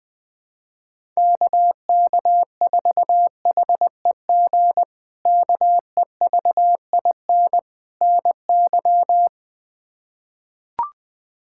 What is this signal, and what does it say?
Morse code, heard as a keyed tone, about 21 wpm, 700 Hz. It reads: KK4HEG KEVIN NY
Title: K1USN SST QSO Practice - 20wpm